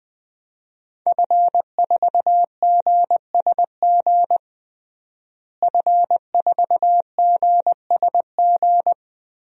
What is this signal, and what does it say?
Morse code, heard as a keyed tone, about 20 wpm, 700 Hz. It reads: F4GSG F4GSG
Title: DXCC-100 Call Signs - 5 Characters - Repeated 3x - 20wpm